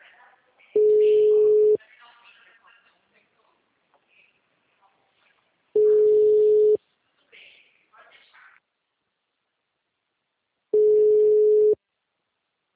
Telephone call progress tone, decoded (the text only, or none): ringback tone